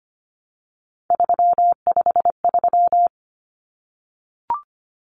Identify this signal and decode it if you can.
Morse code, heard as a keyed tone, about 25 wpm, 700 Hz. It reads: 353
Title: Numbers - 3-Digit 25wpm